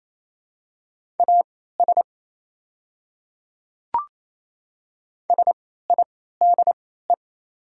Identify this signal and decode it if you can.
Morse code, heard as a keyed tone, about 28 wpm, spaced as for 16 wpm, 700 Hz. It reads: AS SIDE
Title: 3x Character Spacing - Top 100 Words - 28wpm